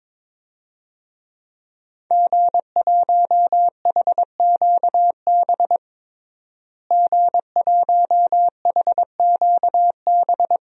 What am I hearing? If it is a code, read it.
Morse code, heard as a keyed tone, about 22 wpm, 700 Hz. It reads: G1HQB G1HQB
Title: DXCC-100 Call Signs - 5 Characters - Repeated 3x - 22wpm